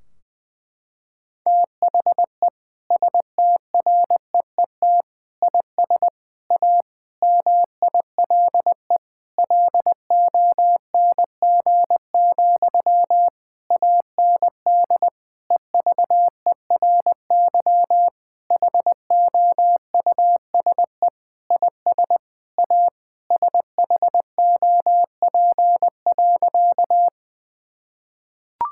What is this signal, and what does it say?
Morse code, heard as a keyed tone, about 20 wpm, 700 Hz. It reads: THE STREET IS A MILE LONG, AND EVERY HOUSE IS A SHOP.
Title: Sentences from Top 1600 Words 20wpm